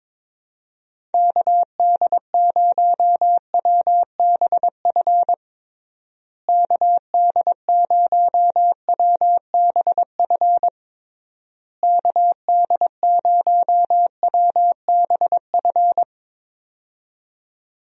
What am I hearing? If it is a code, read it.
Morse code, heard as a keyed tone, about 22 wpm, 700 Hz. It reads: KD0WBF KD0WBF KD0WBF
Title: USA Call Signs - 6 Characters - Repeated 3x - 22wpm